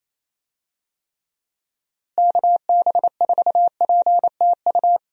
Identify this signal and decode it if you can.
Morse code, heard as a keyed tone, about 28 wpm, 700 Hz. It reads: KB4PTU
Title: USA Call Signs - 6 Characters - Repeated 2x - 28wpm